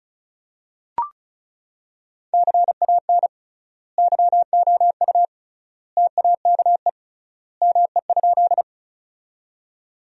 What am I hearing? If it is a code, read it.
Morse code, heard as a keyed tone, about 34 wpm, 700 Hz. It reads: CAN YOU TAKE ME?
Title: 3x Word-Spacing - Sentences from Top 200 Words - 35wpm